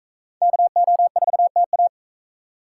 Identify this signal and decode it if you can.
Morse code, heard as a keyed tone, about 44 wpm, 700 Hz. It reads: KO4TA